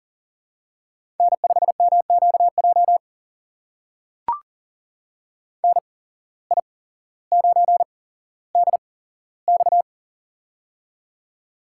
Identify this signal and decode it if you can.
Morse code, heard as a keyed tone, about 39 wpm, spaced as for 17 wpm, 700 Hz. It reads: N5MQJ NI9DX
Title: USA Call Signs - 5 Characters - 8x Character Spacing - 40wpm